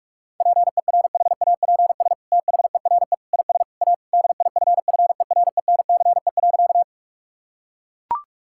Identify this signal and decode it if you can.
Morse code, heard as a keyed tone, about 41 wpm, 700 Hz. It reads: PERHAPS THERE IS A DIFFERENCE.
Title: Sentences from Top 1000 Words - Review Part 5 - 45wpm